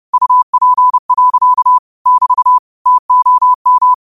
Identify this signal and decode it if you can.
Morse code, heard as a keyed tone, about 30 wpm, 1000 Hz. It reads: UP. <BT> TOM
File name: qotdfiles / 30wpm